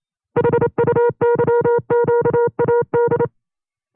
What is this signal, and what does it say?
Morse code, heard as a keyed tone, about 28 wpm, 480 Hz. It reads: HUYQAD
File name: FreeFiles / CW-practice / 25WPM-side1